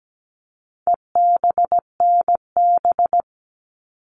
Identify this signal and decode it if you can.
Morse code, heard as a keyed tone, about 17 wpm, 700 Hz. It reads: EBNB
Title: All Cut Numbers - 4-Digit 17wpm